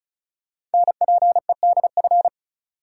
Morse code, heard as a keyed tone, about 36 wpm, 700 Hz. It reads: NPEDF
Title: Code Groups - 5 Letter:Numbers - 35wpm